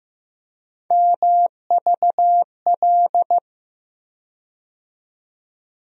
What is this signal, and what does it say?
Morse code, heard as a keyed tone, about 15 wpm, 700 Hz. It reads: MVL
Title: USA Call Sign Postfixes - 3 Letters 15wpm